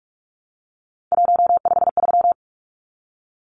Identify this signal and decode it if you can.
Morse code, heard as a keyed tone, about 44 wpm, 700 Hz. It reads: 153
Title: Numbers - 3-Digit 45wpm